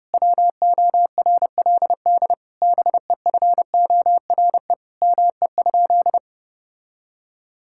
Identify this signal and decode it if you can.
Morse code, heard as a keyed tone, about 32 wpm, 700 Hz. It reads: WORLD BEFORE ME?
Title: Sentences from Top 1000 Words - Review Part 5 - 30wpm